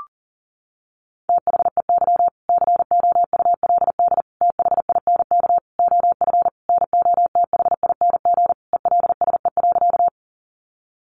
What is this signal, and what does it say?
Morse code, heard as a keyed tone, about 40 wpm, 700 Hz. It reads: THEY COULD THINK OF NOTHING ELSE.
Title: Sentences from Top 800 Words 40wpm